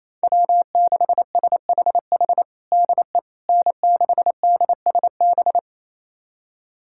Morse code, heard as a keyed tone, about 28 wpm, 700 Hz. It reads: W6SHH DE N6DSB
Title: USA Callsign DE Callsign - 5 Characters - 28wpm